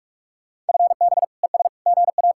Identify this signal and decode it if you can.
Morse code, heard as a keyed tone, about 59 wpm, 700 Hz. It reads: FB ES GA